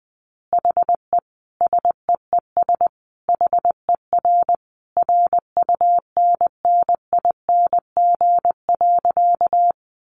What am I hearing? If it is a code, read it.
Morse code, heard as a keyed tone, about 20 wpm, 700 Hz. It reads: HE SEES HER RUNNING.
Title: Additional Easy Phrases 20wpm